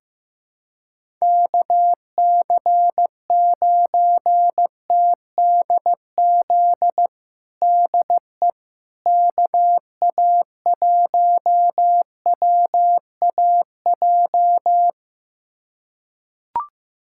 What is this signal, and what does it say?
Morse code, heard as a keyed tone, about 15 wpm, 700 Hz. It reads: KC9TDZ DE KA1WAJ
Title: USA Callsign DE Callsign - 6 Characters - 15wpm